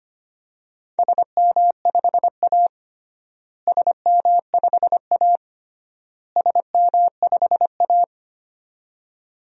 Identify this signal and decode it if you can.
Morse code, heard as a keyed tone, about 25 wpm, 700 Hz. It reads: SM5A SM5A SM5A
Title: DXCC-100 Call Signs - 4 Characters - Repeated 3x - 25wpm